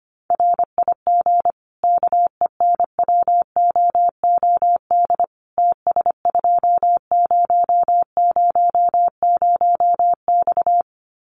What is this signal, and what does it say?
Morse code, heard as a keyed tone, about 25 wpm, 700 Hz. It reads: RIG KENWOOD TS2000X
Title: Common QSO Phrases 25wpm